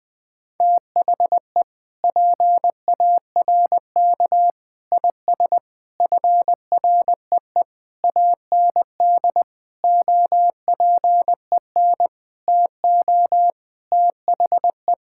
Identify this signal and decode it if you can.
Morse code, heard as a keyed tone, about 20 wpm, 700 Hz. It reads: THE PARK IS FREE AND OPEN TO THE
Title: Sentences with 7-Word Phrases - Spoken Beginning - 20wpm